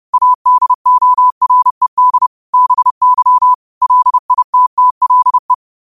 Morse code, heard as a keyed tone, about 30 wpm, 1000 Hz. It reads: ADORED BY LITTLE
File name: qotdfiles / 30wpm